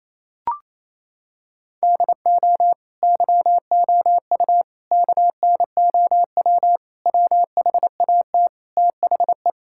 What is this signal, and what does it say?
Morse code, heard as a keyed tone, about 27 wpm, 700 Hz. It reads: DO YOU KNOW WHAT THE
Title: Sentences from Top 1000 Words - Review Part 6 - 28wpm